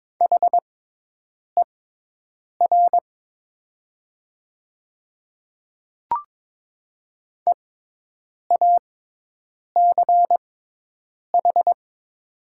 Morse code, heard as a keyed tone, about 22 wpm, spaced as for 8 wpm, 700 Hz. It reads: HER EACH
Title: 6x Character Spacing - Top 100 Words - 22wpm